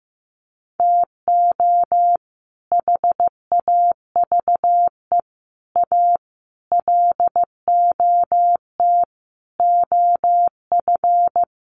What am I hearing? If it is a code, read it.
Morse code, heard as a keyed tone, about 15 wpm, 700 Hz. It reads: TO HAVE A LOT OF
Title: Sets of 5 Words Encore 15wpm